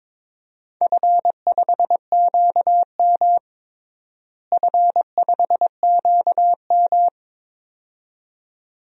Morse code, heard as a keyed tone, about 21 wpm, 700 Hz. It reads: F5QM F5QM
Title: DXCC-100 Call Signs - 4 Characters - Repeated 2x - 22wpm